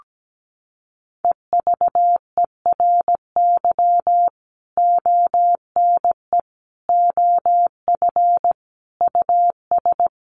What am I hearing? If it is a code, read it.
Morse code, heard as a keyed tone, about 17 wpm, 700 Hz. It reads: EVERY ONE OF US
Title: Sentences from Top 1500 Words 17wpm